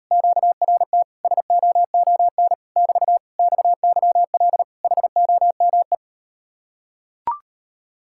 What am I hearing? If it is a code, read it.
Morse code, heard as a keyed tone, about 38 wpm, 700 Hz. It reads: QRT SOON <BT> XYL HOME